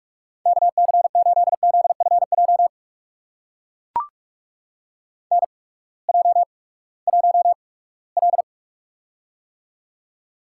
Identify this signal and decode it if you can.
Morse code, heard as a keyed tone, about 44 wpm, spaced as for 22 wpm, 700 Hz. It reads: KC8ZFJ NJ1L
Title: USA Call Signs - 8x Character Spacing - 45wpm